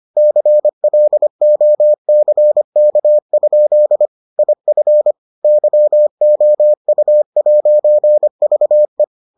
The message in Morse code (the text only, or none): CLOCK? IF YOU'VE